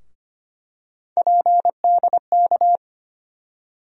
Morse code, heard as a keyed tone, about 25 wpm, 700 Hz. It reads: PDK
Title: USA Call Sign Postfixes - 3 Letters 25wpm